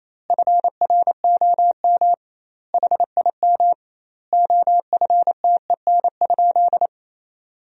Morse code, heard as a keyed tone, about 28 wpm, 700 Hz. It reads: FROM HIM OFTEN?
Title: 2x Word-Spacing - Sentences from Top 400 Words - 28wpm